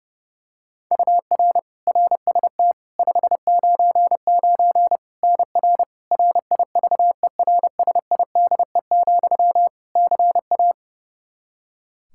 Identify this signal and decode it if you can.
Morse code, heard as a keyed tone, about 32 wpm, 700 Hz. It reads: UR RST 599 NR RIVERSIDE, CA
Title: Practice QSO - Exchange 1 - Beginning - 30wpm